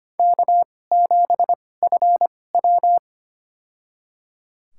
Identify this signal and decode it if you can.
Morse code, heard as a keyed tone, about 25 wpm, spaced as for 18 wpm, 700 Hz. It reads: K7FW
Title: USA Call Signs - 4 Characters - 2x Character Spacing - 25wpm